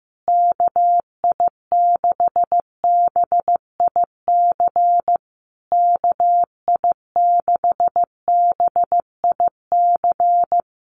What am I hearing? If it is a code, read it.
Morse code, heard as a keyed tone, about 15 wpm, 700 Hz. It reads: KI6BIC KI6BIC
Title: USA Call Signs Repeated 3x 15wpm